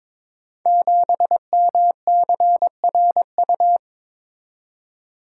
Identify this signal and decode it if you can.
Morse code, heard as a keyed tone, about 23 wpm, 700 Hz. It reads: 7MCRU
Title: Code Groups - 5 Letter:Numbers - 22wpm